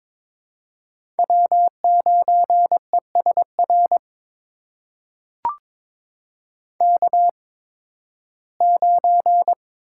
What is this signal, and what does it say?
Morse code, heard as a keyed tone, about 22 wpm, spaced as for 14 wpm, 700 Hz. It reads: W9ESR K9
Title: USA Call Signs - 5 Characters - 8x Character Spacing - 22wpm